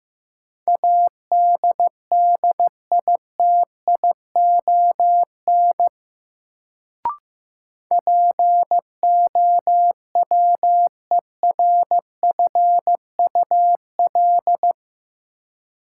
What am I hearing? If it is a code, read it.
Morse code, heard as a keyed tone, about 15 wpm, 700 Hz. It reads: ADDITION POWERFUL
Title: Sentences from Top 1000 Words - Review Part 6 - New Word Drill - 15wpm